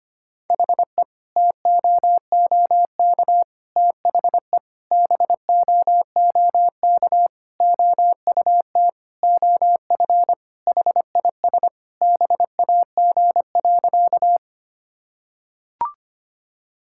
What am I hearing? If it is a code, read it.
Morse code, heard as a keyed tone, about 25 wpm, 700 Hz. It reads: HE TOOK THE BOOK OUT OF HIS BAG.
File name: practice / 25wpm / Sentences with 3-Word Phrases - 25wpm